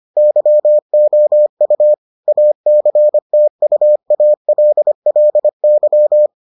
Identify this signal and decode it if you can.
Morse code, heard as a keyed tone, about 25 wpm, 600 Hz. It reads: YOU ACTUALLY